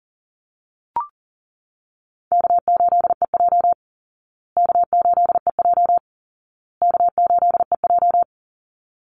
Morse code, heard as a keyed tone, about 40 wpm, 700 Hz. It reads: K8EJ K8EJ K8EJ
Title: Easy Call Signs Repeated 3x - 40wpm